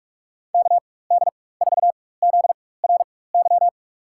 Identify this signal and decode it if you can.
Morse code, heard as a keyed tone, about 46 wpm, spaced as for 21 wpm, 700 Hz. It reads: KD4ZRY
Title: USA Call Signs - 4x Character Spacing - 45wpm